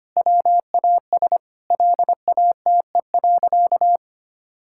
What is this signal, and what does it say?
Morse code, heard as a keyed tone, about 25 wpm, 700 Hz. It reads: WAS LATE.
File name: practice / 25wpm / Sentences with 3-Word Phrases - 25wpm